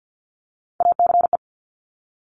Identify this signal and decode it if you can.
Morse code, heard as a keyed tone, about 50 wpm, 700 Hz. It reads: ACE
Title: Top Three Letter Combinations in Words 50wpm